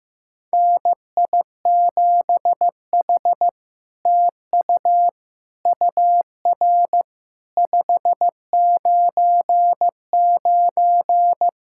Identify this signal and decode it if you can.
Morse code, heard as a keyed tone, about 15 wpm, 700 Hz. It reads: NI7H TU UR 599
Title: SOTA QSO Practice 15wpm